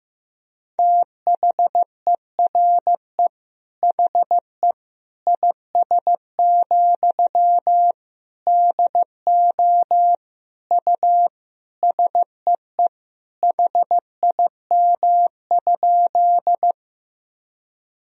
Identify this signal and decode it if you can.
Morse code, heard as a keyed tone, about 15 wpm, 700 Hz. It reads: THERE HE IS, DO U SEE HIM?
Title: Sentences from Top 100 Words and Abbreviations 15wpm